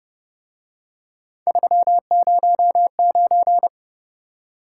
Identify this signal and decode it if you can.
Morse code, heard as a keyed tone, about 29 wpm, 700 Hz. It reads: 309